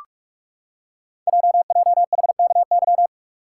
Morse code, heard as a keyed tone, about 44 wpm, 700 Hz. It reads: JJHKY
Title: Code Groups - 5 Letters - 45wpm